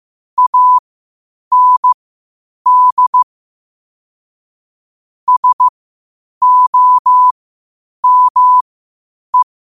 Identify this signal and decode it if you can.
Morse code, heard as a keyed tone, about 15 wpm, spaced as for 8 wpm, 1000 Hz. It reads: AND SOME
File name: qotdfiles / 8wpm